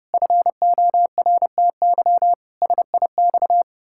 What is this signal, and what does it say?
Morse code, heard as a keyed tone, about 31 wpm, 700 Hz. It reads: FORTY SIX
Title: Pangram Challenge v2 - 30wpm